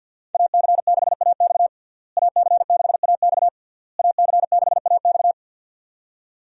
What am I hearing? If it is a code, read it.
Morse code, heard as a keyed tone, about 51 wpm, 700 Hz. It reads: AC6AX AC6AX AC6AX
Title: USA Call Signs - 5 Characters - Repeated 3x - 50wpm